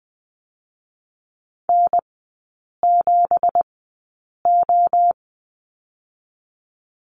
Morse code, heard as a keyed tone, about 20 wpm, spaced as for 8 wpm, 700 Hz. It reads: N7O
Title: 184 Lesson 30e - Letter b - Three Characters 20wpm